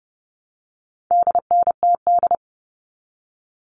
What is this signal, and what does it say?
Morse code, heard as a keyed tone, about 30 wpm, 700 Hz. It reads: DNTD
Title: All Cut Numbers - 4-Digit 30wpm